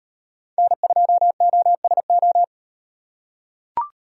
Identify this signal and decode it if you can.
Morse code, heard as a keyed tone, about 38 wpm, 700 Hz. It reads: N2OSO